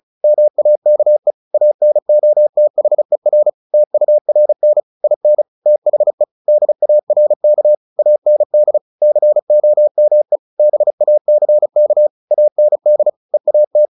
Morse code, heard as a keyed tone, about 34 wpm, 600 Hz. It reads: MAKE ANOTHER TURN IN THE DARK AND COME BACK AND EAT
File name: the old man and the sea / the old man and the sea 1 35wpm